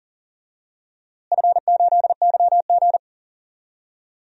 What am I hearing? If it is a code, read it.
Morse code, heard as a keyed tone, about 40 wpm, 700 Hz. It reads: F8YG
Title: DXCC-100 Call Signs - 4 Characters - Repeated 3x - 40wpm